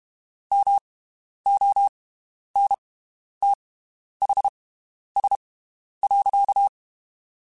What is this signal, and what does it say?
Morse code, heard as a keyed tone, about 32 wpm, spaced as for 11 wpm, 800 Hz. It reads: MONTHS.
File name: (spaced) / QSOs Spaced-mp3s / WD0BGP 1 @30 sp